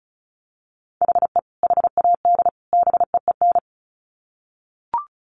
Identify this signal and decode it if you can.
Morse code, heard as a keyed tone, about 34 wpm, 700 Hz. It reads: HE HAD BEEN
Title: Sets of 3 Words 35wpm